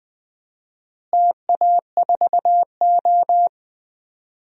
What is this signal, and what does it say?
Morse code, heard as a keyed tone, about 20 wpm, 700 Hz. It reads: TA4O
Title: DXCC-100 Call Signs - 4 Characters - Repeated 3x - 20wpm